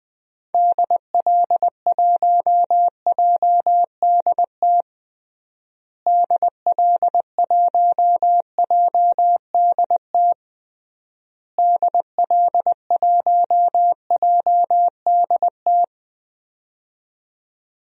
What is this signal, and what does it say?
Morse code, heard as a keyed tone, about 21 wpm, 700 Hz. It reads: DL1JDT DL1JDT DL1JDT
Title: DXCC-100 Call Signs - 6 Characters - Repeated 3x - 20wpm